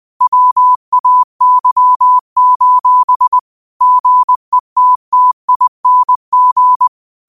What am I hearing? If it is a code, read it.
Morse code, heard as a keyed tone, about 20 wpm, 1000 Hz. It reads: WAY: GETTING